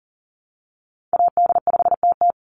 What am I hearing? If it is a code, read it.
Morse code, heard as a keyed tone, about 40 wpm, 700 Hz. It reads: AD5TT
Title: Call Signs Repeated 3x - 40wpm